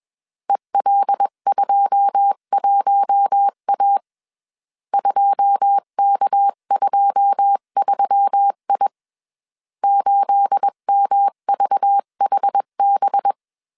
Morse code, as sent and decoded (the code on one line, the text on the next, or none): . .-.. ..--- .---- .- / ..--- -.- ..--- ...-- .. / ---.. -- ...- .... -...
EL21A 2K23I 8MVHB